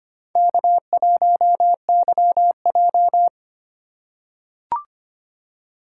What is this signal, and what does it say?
Morse code, heard as a keyed tone, about 25 wpm, 700 Hz. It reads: K1YJ
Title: USA Call Signs - 4 Characters - 4x Character Spacing - 25wpm